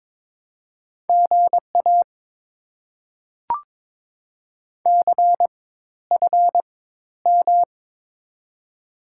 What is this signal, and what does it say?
Morse code, heard as a keyed tone, about 22 wpm, spaced as for 12 wpm, 690 Hz. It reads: GA CFM
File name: Top 90 QSO Elements - 4x Character Spacing - 22wpm